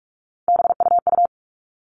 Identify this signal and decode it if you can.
Morse code, heard as a keyed tone, about 44 wpm, 700 Hz. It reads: BUU
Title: All Cut Numbers - 3-Digit 45wpm